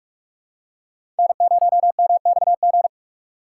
Morse code, heard as a keyed tone, about 42 wpm, 700 Hz. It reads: N0MXG
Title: USA Call Signs - 3x Character Spacing - 45wpm